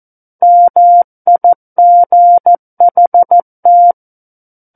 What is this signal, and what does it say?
Morse code, heard as a keyed tone, about 14 wpm, 700 Hz. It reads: MIGHT
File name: Book One / 05-Chapter 5 - 14 WPM